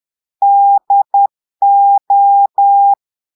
Morse code, heard as a keyed tone, about 10 wpm, 800 Hz. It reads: DO
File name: portugais texte en clair 2 vitesse 10 mots minutes